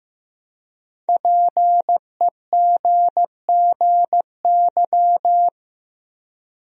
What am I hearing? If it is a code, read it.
Morse code, heard as a keyed tone, about 15 wpm, 700 Hz. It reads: PEGGY
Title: USA Common Names - 15wpm